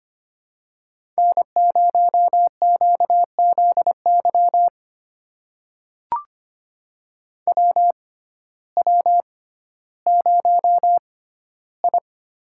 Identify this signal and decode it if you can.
Morse code, heard as a keyed tone, about 25 wpm, spaced as for 14 wpm, 700 Hz. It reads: N0QZY WW0I
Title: USA Call Signs - 6x Character Spacing - 25wpm